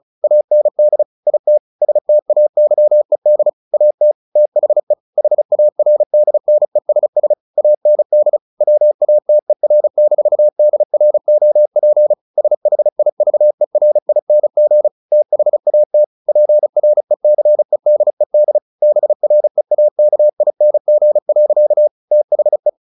Morse code, heard as a keyed tone, about 36 wpm, 600 Hz. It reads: AND IT STAYED AT THE HARDNESS AND WATER-DROP SHIVERING THAT PRECEDED BREAKING. THE
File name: the old man and the sea 2 35wpm